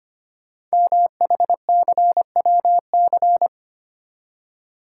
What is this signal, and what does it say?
Morse code, heard as a keyed tone, about 25 wpm, 700 Hz. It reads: MHCWC
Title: Code Groups - 5 Letter:Numbers - 25wpm